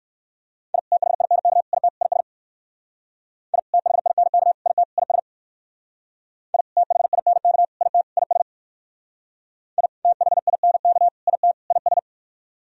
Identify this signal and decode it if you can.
Morse code, heard as a keyed tone, about 57 wpm, 700 Hz. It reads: I THINK IT IS I THINK IT IS I THINK IT IS I THINK IT IS